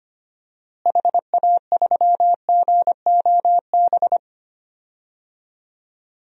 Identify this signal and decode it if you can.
Morse code, heard as a keyed tone, about 25 wpm, 700 Hz. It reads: HA3GOB
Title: DXCC-100 Call Signs - 6 Characters - Repeated 3x - 25wpm